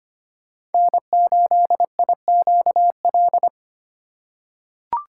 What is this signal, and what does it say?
Morse code, heard as a keyed tone, about 25 wpm, 700 Hz. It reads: N8IQL
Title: USA Call Signs - 5 Characters - 3x Character Spacing - 25wpm